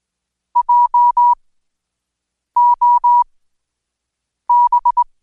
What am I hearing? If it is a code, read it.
Morse code, heard as a keyed tone, about 21 wpm, spaced as for 6 wpm, 970 Hz. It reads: JOB
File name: Code Quick/AudioDownld2/Seg109-Day14-DayDrill 14